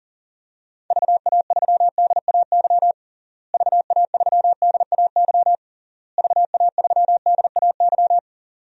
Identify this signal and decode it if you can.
Morse code, heard as a keyed tone, about 40 wpm, 700 Hz. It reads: VA3DAY VA3DAY VA3DAY
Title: DXCC-100 Call Signs - 6 Characters - Repeated 3x - 40wpm